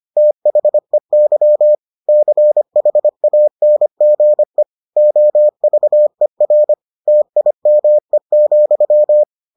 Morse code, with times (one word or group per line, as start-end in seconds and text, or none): 0.2-1.8 THEY
2.1-4.6 CHANGE
5.0-6.7 OVER
7.1-9.2 TIME,